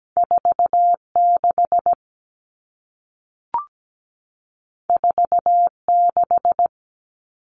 Morse code, heard as a keyed tone, about 17 wpm, 700 Hz. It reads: 46 46
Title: All Cut Numbers - 2-Digit 17wpm